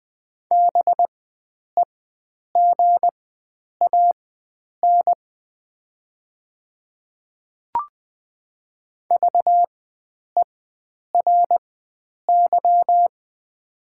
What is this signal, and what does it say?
Morse code, heard as a keyed tone, about 20 wpm, spaced as for 9 wpm, 700 Hz. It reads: BEGAN VERY
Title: 4x Character Spacing - Top 500 Words - 20wpm